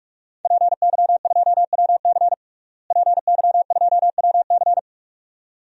Morse code, heard as a keyed tone, about 46 wpm, 700 Hz. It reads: PY2WC PY2WC
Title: DXCC-100 Call Signs - 5 Characters - Repeated 2x - 45wpm